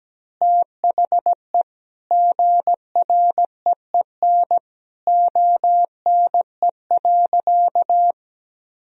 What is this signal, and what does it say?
Morse code, heard as a keyed tone, about 17 wpm, 700 Hz. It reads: THE GREEN ONE.
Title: Sentences from Top 1000 Words - Review Part 2 - 17wpm